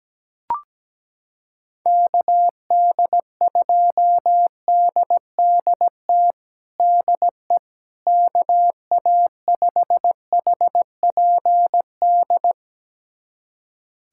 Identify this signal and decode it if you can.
Morse code, heard as a keyed tone, about 17 wpm, 700 Hz. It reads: KD2DDT DE KA5HPD
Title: USA Callsign DE Callsign - 6 Characters - 17wpm